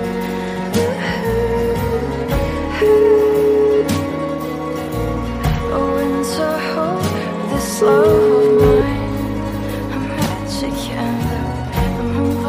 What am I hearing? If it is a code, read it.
Ringback tone (Europe, 425 Hz), with noise.